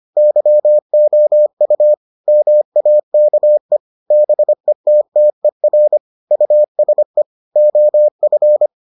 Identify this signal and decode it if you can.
Morse code, heard as a keyed tone, about 25 wpm, 600 Hz. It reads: YOU MAKE BETTER USE OF